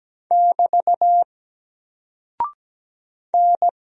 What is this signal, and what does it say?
Morse code, heard as a keyed tone, about 17 wpm, 700 Hz. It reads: <BT> N
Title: Single Character (Letter:Number:Punctuation:Prosign) 17wpm